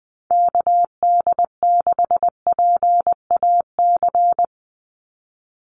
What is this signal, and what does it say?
Morse code, heard as a keyed tone, about 20 wpm, 700 Hz. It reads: KD6PAC
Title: USA Call Signs 20wpm